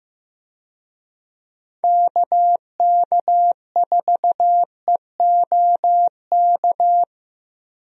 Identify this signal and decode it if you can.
Morse code, heard as a keyed tone, about 15 wpm, 700 Hz. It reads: KK4EOK
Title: USA Call Signs - 6 Characters - 15wpm